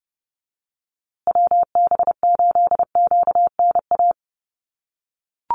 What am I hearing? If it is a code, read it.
Morse code, heard as a keyed tone, about 30 wpm, 700 Hz. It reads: WB8QNA
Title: USA Call Signs Repeated 3x 30wpm